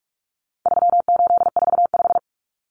Morse code, heard as a keyed tone, about 44 wpm, 700 Hz. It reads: 3845
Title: Common Cut Numbers - 4-Digit 45wpm